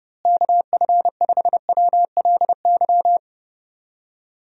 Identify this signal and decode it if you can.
Morse code, heard as a keyed tone, about 30 wpm, 700 Hz. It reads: KF5WLY
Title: USA Call Signs - 6 Characters - Repeated 3x - 30wpm